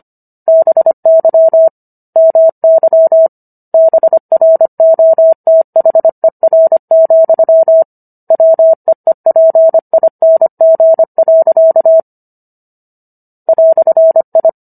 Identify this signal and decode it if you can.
Morse code, heard as a keyed tone, about 25 wpm, 650 Hz. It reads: BY MY BROTHER, WEEPING. "I